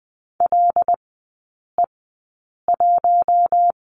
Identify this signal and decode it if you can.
Morse code, heard as a keyed tone, about 20 wpm, spaced as for 8 wpm, 700 Hz. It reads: LE1
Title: 074 Lesson 13e - Letter L - Three Characters 20wpm